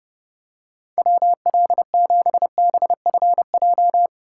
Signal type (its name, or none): Morse code